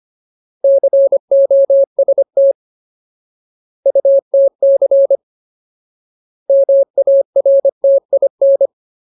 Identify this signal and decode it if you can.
Morse code, heard as a keyed tone, about 25 wpm, 550 Hz. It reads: COST UTC MARTIN
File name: Ultimate Word List / 25WPM Ultimate